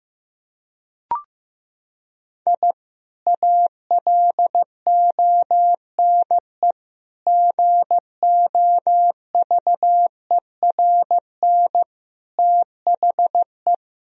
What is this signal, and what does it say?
Morse code, heard as a keyed tone, about 15 wpm, 700 Hz. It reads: I ALONE GOVERN THE
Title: Sentences from Top 1000 Words - Review Part 2 - 15wpm